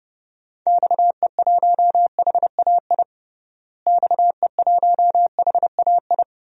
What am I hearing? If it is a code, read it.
Morse code, heard as a keyed tone, about 30 wpm, 700 Hz. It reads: XE1HAI XE1HAI
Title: DXCC-100 Call Signs - 6 Characters - Repeated 3x - 30wpm